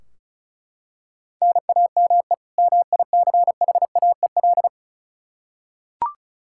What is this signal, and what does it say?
Morse code, heard as a keyed tone, about 36 wpm, 700 Hz. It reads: NAME MICHAEL
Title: Practice QSO - Exchange 1 - Beginning - 35wpm